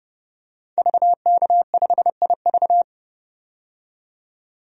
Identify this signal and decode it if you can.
Morse code, heard as a keyed tone, about 30 wpm, 700 Hz. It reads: VK5IV
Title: DXCC-100 Call Signs - 5 Characters - 30wpm